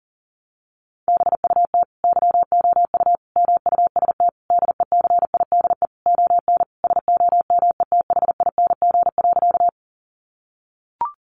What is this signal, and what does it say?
Morse code, heard as a keyed tone, about 40 wpm, 700 Hz. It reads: BUT YOU MUST DECIDE ON SOMETHING.
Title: Sentences from Top 600 Words 40wpm